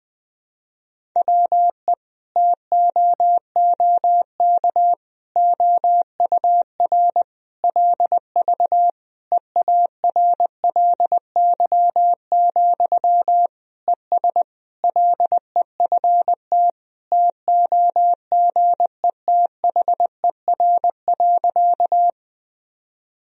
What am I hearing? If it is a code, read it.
Morse code, heard as a keyed tone, about 20 wpm, 700 Hz. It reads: WE TOOK OUR LV EARLY, ES LEFT TOGETHER.
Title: Sentences from Top 400 Words and Abbreviations 20wpm